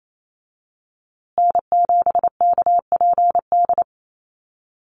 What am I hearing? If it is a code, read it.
Morse code, heard as a keyed tone, about 28 wpm, 700 Hz. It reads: N7KPD
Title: USA Call Signs Repeated 3x 28wpm